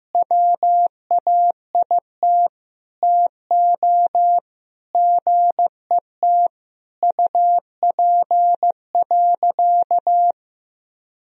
Morse code, heard as a keyed tone, about 16 wpm, 700 Hz. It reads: WAIT TO GET UP.
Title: Sentences from Top 500 Words and Abbreviations 15wpm